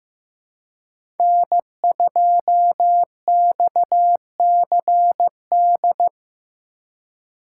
Morse code, heard as a keyed tone, about 15 wpm, 700 Hz. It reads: N2XCD
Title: USA Call Signs - 5 Characters - 3x Character Spacing - 15wpm